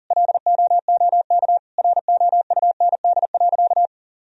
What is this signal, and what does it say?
Morse code, heard as a keyed tone, about 40 wpm, 700 Hz. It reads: LOOK ROUND.
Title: Sentences from Top 1000 Words - Review Part 1 - 40wpm